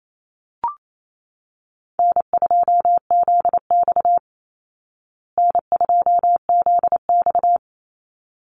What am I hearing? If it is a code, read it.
Morse code, heard as a keyed tone, about 28 wpm, 700 Hz. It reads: N2ZX N2ZX
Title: Easy Call Signs Repeated 2x - 28wpm